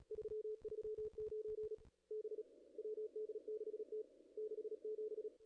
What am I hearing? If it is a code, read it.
Morse code, heard as a keyed tone, about 33 wpm, 440 Hz. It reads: 339 B WD6T 67
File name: contests / ss-cw-k5tr-op-n5rz -0000